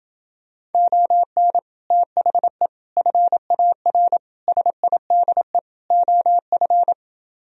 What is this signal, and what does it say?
Morse code, heard as a keyed tone, about 28 wpm, 700 Hz. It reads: ON THE FAR SIDE OF